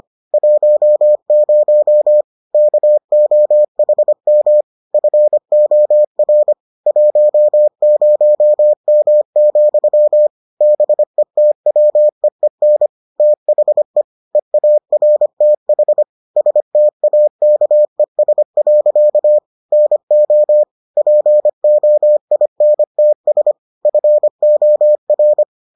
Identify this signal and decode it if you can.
Morse code, heard as a keyed tone, about 25 wpm, 600 Hz. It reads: 10 KOHM FOR 10M, BETWEEN THE EARTH STAKES. NO POINTS FOR